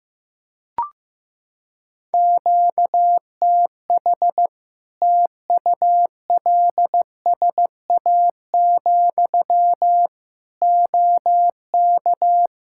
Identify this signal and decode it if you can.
Morse code, heard as a keyed tone, about 15 wpm, 700 Hz. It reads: QTH TULSA, OK
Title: Practice QSO - Exchange 1 - Beginning - 15wpm